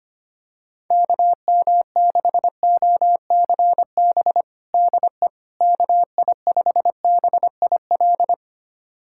Morse code, heard as a keyed tone, about 25 wpm, 700 Hz. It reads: KM6OCB DE KI5BIL
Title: USA Callsign DE Callsign - 6 Characters - 25wpm